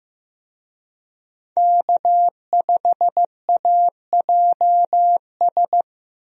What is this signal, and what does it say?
Morse code, heard as a keyed tone, about 15 wpm, 700 Hz. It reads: K5AJS
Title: USA Call Signs - 5 Characters - Repeated 3x - 15wpm